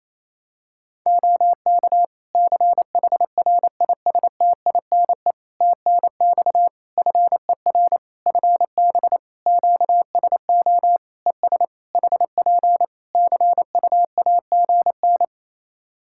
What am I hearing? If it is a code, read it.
Morse code, heard as a keyed tone, about 28 wpm, 700 Hz. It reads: OK CHRISTINE TNX FER FB QSO ES HP CUAGN
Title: Practice QSO - Exchange 4 - Ending - 28wpm